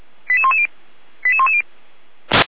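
Five-tone selective call (ZVEI), heard as quick runs of five tones, tones of about 70 ms. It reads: 88110 88110
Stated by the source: ZVEI five-tone selective call 88105 received via SDR, recorded by Feuerwehr Orlamünde.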